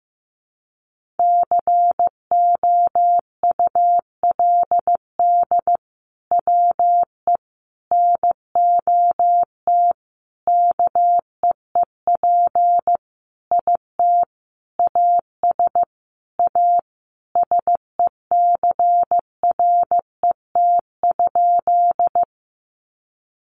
Morse code, heard as a keyed tone, about 15 wpm, 700 Hz. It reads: COULD WE NOT KEEP IT AS A SECRET?
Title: Sentences from Top 1200 Words 15wpm